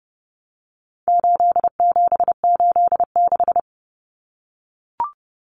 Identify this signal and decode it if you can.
Morse code, heard as a keyed tone, about 30 wpm, 700 Hz. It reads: 8786
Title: Numbers - 4-Digit 30wpm